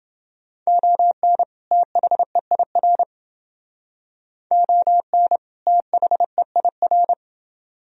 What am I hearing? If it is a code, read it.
Morse code, heard as a keyed tone, about 28 wpm, 700 Hz. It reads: ON THEIR ON THEIR